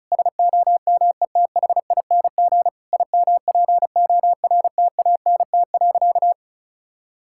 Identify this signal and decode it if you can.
Morse code, heard as a keyed tone, about 36 wpm, 700 Hz. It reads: SOMETHING IMPORTANT.
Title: Sentences from Top 1000 Words - Review Part 6 - 35wpm